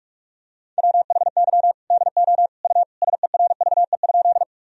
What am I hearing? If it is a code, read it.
Morse code, heard as a keyed tone, about 41 wpm, 700 Hz. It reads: WHY DO U SERVE?
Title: Sentences from Top 400 Words and Abbreviations 45wpm